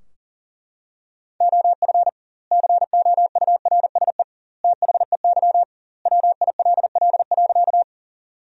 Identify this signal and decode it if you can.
Morse code, heard as a keyed tone, about 40 wpm, 700 Hz. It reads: OF COURSE THEY WILL.
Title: 2x Word-Spacing - Sentences from Top 500 Words - 40wpm